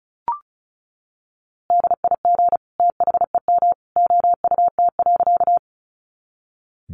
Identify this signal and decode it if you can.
Morse code, heard as a keyed tone, about 34 wpm, 700 Hz. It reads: DIG THEM OUT.
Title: A Sound Of Thunder 35wpm